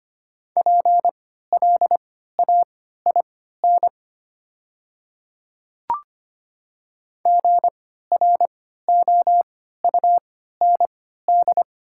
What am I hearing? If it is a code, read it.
Morse code, heard as a keyed tone, about 25 wpm, spaced as for 14 wpm, 690 Hz. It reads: PLAIN GROUND
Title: 3x Character Spacing - Top 500 Words - 25wpm